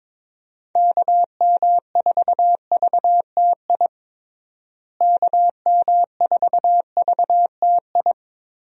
Morse code, heard as a keyed tone, about 22 wpm, 700 Hz. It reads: KM4VTI KM4VTI
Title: USA Call Signs - 6 Characters - Repeated 2x - 22wpm